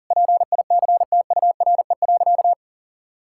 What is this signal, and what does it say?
Morse code, heard as a keyed tone, about 37 wpm, 700 Hz. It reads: PICTURE.
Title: Sentences from Top 1000 Words - Review Part 6 - 40wpm